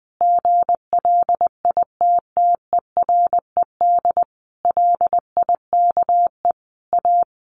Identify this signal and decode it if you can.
Morse code, heard as a keyed tone, about 20 wpm, 700 Hz. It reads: GLITTERED LIKE A
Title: A Sound Of Thunder 20wpm